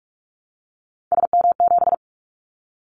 Morse code, heard as a keyed tone, about 46 wpm, 700 Hz. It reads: SM7
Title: Callsign Prefixes for Top 25 DXCC - 45wpm